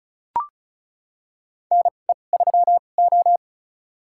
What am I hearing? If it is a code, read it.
Morse code, heard as a keyed tone, about 36 wpm, spaced as for 25 wpm, 700 Hz. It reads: NE3O
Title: USA Call Signs - 4 Characters - 2x Character Spacing - 35wpm